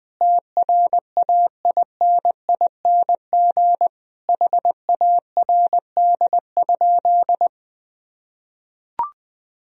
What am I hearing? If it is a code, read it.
Morse code, heard as a keyed tone, about 21 wpm, 700 Hz. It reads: TRAINING HARD?
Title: Sentences from Top 1000 Words - Review Part 4 - 20wpm